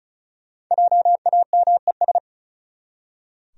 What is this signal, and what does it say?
Morse code, heard as a keyed tone, about 34 wpm, 700 Hz. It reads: JAMES
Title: USA Common Names - 35wpm